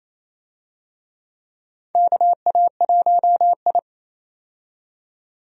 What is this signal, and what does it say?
Morse code, heard as a keyed tone, about 28 wpm, 700 Hz. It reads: KA1I
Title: USA Call Signs - 4 Characters - 28wpm